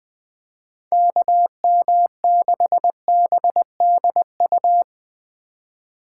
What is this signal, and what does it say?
Morse code, heard as a keyed tone, about 20 wpm, 700 Hz. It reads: KM6BDU
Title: Speed Racing - USA 2x3 Call Signs 17wpm